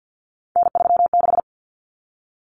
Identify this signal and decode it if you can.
Morse code, heard as a keyed tone, about 50 wpm, 700 Hz. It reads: N36
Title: Common Cut Numbers - 3-Digit 50wpm